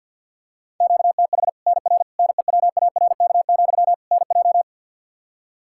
Morse code, heard as a keyed tone, about 52 wpm, 700 Hz. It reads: QTH NR NEWARK, NJ
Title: Practice QSO - Exchange 1 - Beginning - 50wpm